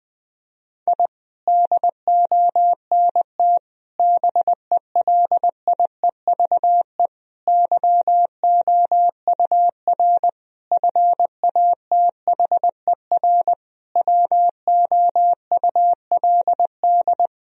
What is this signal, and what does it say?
Morse code, heard as a keyed tone, about 20 wpm, 700 Hz. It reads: I DONT BELIEVE YOUR FATHER WOULD
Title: Sentences from Top 1000 Words - Review Part 3 - 20wpm